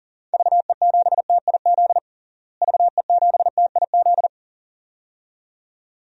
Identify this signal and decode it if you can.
Morse code, heard as a keyed tone, about 40 wpm, 700 Hz. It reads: VE7TIZ VE7TIZ
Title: DXCC-100 Call Signs - 6 Characters - Repeated 2x - 40wpm